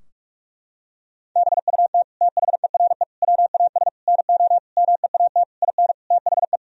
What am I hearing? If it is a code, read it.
Morse code, heard as a keyed tone, about 46 wpm, 700 Hz. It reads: BUT THERE WAS NO MEAT IN THE
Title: Sentences from Top 1000 Words - Review Part 3 - 45wpm